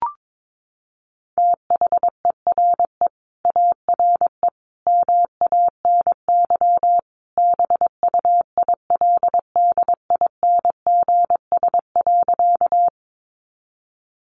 Morse code, heard as a keyed tone, about 22 wpm, 700 Hz. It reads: THERE ARE MANY BUILDINGS.
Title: Easy Phrases 22wpm